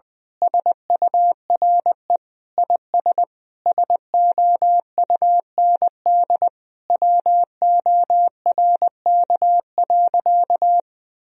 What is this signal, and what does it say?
Morse code, heard as a keyed tone, about 20 wpm, 700 Hz. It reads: SURE IS SOUND WORK.
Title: Sentences from Top 1000 Words - Review Part 2 - 20wpm